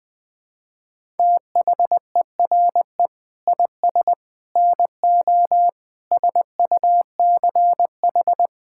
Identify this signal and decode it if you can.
Morse code, heard as a keyed tone, about 20 wpm, 700 Hz. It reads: THERE IS NO SUCH
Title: Sets of 9 Words - 20wpm